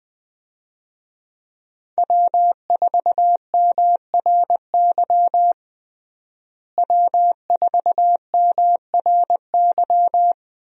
Morse code, heard as a keyed tone, about 20 wpm, 700 Hz. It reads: W4MRY W4MRY
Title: USA Call Signs - 5 Characters - Repeated 2x - 20wpm